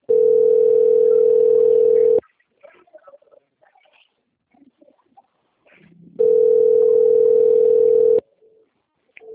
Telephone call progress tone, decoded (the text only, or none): ringback tone